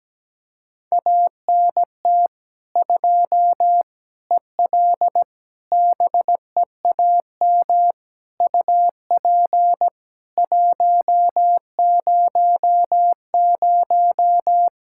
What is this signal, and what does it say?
Morse code, heard as a keyed tone, about 17 wpm, 700 Hz. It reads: ANT 2 EL BEAM UP 100
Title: K1USN SST QSO Practice 17wpm